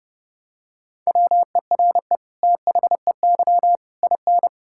Morse code, heard as a keyed tone, about 30 wpm, 700 Hz. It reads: WERE THEY IN
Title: Sentences from Top 1000 Words - Review Part 6 - 30wpm